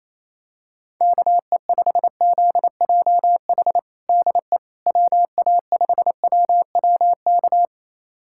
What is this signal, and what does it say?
Morse code, heard as a keyed tone, about 28 wpm, 700 Hz. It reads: KE5ZJH DE WA5WWK
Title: USA Callsign DE Callsign - 6 Characters - 28wpm